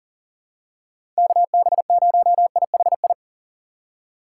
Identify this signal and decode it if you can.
Morse code, heard as a keyed tone, about 40 wpm, 700 Hz. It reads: KB0IHI
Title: USA Call Signs - 6 Characters - 6x Character Spacing - 40wpm